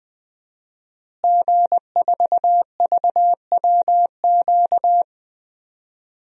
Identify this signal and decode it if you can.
Morse code, heard as a keyed tone, about 20 wpm, 700 Hz. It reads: G4VWQ
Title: DXCC-100 Call Signs - 5 Characters - Repeated 2x - 20wpm